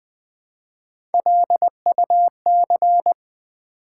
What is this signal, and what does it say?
Morse code, heard as a keyed tone, about 20 wpm, 700 Hz. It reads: LUC
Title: Build A Word - 1 Letter At A Time - 5-Letter Words - 20wpm